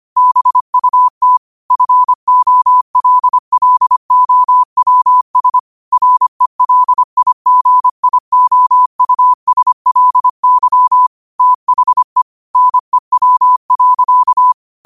Morse code, heard as a keyed tone, about 25 wpm, 1000 Hz. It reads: BUT FOLLOWS RELIGIOUSLY THE NEW.
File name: qotdfiles / 25wpm